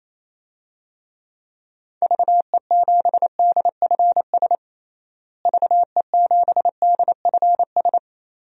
Morse code, heard as a keyed tone, about 28 wpm, 700 Hz. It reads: VE7DFS VE7DFS
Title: DXCC-100 Call Signs - 6 Characters - Repeated 3x - 28wpm